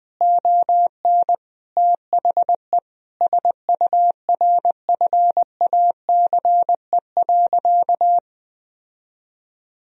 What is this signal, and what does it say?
Morse code, heard as a keyed tone, about 20 wpm, 700 Hz. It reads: ON THE SURFACE.
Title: Sentences from Top 1000 Words - Review Part 3 - 20wpm